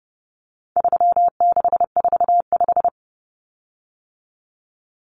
Morse code, heard as a keyed tone, about 30 wpm, 700 Hz. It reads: 3645
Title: Common Cut Numbers - 4-Digit 30wpm